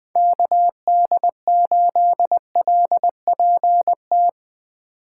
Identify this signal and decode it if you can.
Morse code, heard as a keyed tone, about 20 wpm, 700 Hz. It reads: KD8LPT
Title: USA Call Signs - 6 Characters - 3x Character Spacing - 20wpm